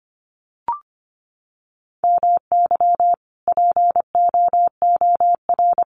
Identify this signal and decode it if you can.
Morse code, heard as a keyed tone, about 25 wpm, 700 Hz. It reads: MY POOR
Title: Sentences from Top 1000 Words 25wpm